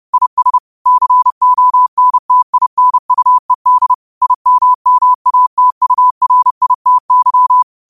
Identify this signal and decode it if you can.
Morse code, heard as a keyed tone, about 30 wpm, 1000 Hz. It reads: IS CONTINUED IMMATURITY
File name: qotdfiles / 30wpm